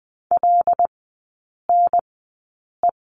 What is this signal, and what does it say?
Morse code, heard as a keyed tone, about 20 wpm, spaced as for 8 wpm, 700 Hz. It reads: LNE
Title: 121 Lesson 20e - Number 3 - Three Characters 20wpm